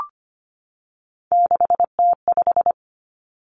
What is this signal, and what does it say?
Morse code, heard as a keyed tone, about 25 wpm, 700 Hz. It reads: 6T5
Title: Three Challenging Letter-Number Combinations 25wpm